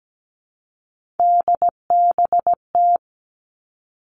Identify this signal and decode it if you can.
Morse code, heard as a keyed tone, about 17 wpm, 700 Hz. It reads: DBT
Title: All Cut Numbers - 3-Digit 17wpm